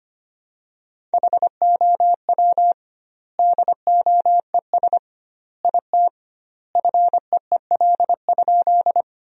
Morse code, heard as a keyed tone, about 25 wpm, 700 Hz. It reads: HOW DOES IT FEEL?
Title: 2x Word-Spacing - Sentences from Top 400 Words - 25wpm